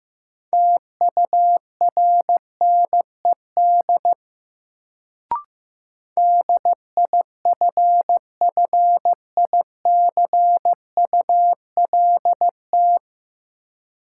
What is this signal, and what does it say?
Morse code, heard as a keyed tone, about 15 wpm, 700 Hz. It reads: TURNED DIFFICULT
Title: Sets of 4 Words - New Word Drill - 15wpm